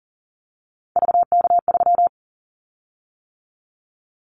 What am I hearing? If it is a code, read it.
Morse code, heard as a keyed tone, about 40 wpm, 700 Hz. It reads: VK3
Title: Callsign Prefixes for Top 100 DXCC - 40wpm